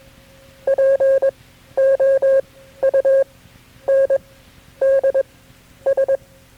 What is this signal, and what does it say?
Morse code, heard as a keyed tone, about 22 wpm, spaced as for 11 wpm, 540 Hz. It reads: POUNDS
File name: KB6MT-tape3-side1